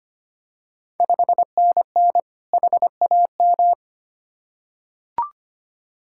Morse code, heard as a keyed tone, about 25 wpm, 700 Hz. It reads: 5NN HAM
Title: Ontario QSO Party Exchange 25wpm